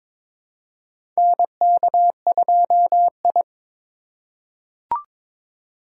Morse code, heard as a keyed tone, about 22 wpm, 700 Hz. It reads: NK2I
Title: USA Call Signs - 4 Characters - 4x Character Spacing - 22wpm